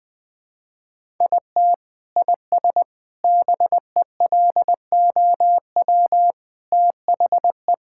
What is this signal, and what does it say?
Morse code, heard as a keyed tone, about 20 wpm, 700 Hz. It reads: IT IS BELOW THE
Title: Sentences from Top 1000 Words - Review Part 5 - 20wpm